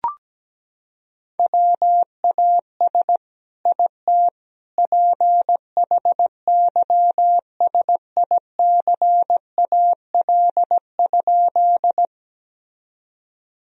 Morse code, heard as a keyed tone, about 17 wpm, 700 Hz. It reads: WAS IT PHYSICAL?
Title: Sentences from Top 1000 Words - Review Part 4 - 17wpm